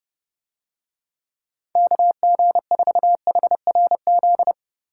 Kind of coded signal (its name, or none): Morse code